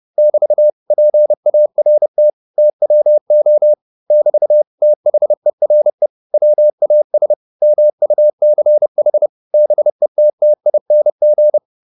Morse code, heard as a keyed tone, about 30 wpm, 600 Hz. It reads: <BT> PART TWO <BT> THERE WAS MUCH BETTING
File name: the old man and the sea 2 30wpm